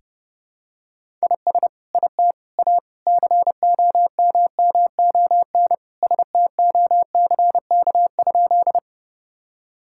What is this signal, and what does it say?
Morse code, heard as a keyed tone, about 30 wpm, 700 Hz. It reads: IS IT A COMMON STOCK?
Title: Sentences from Top 1000 Words - Review Part 5 - 30wpm